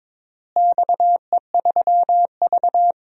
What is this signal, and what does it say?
Morse code, heard as a keyed tone, about 22 wpm, 700 Hz. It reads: XE3V
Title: DXCC-100 Call Signs - 4 Characters - Repeated 2x - 22wpm